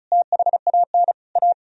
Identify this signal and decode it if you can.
Morse code, heard as a keyed tone, about 46 wpm, 700 Hz. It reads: THAN A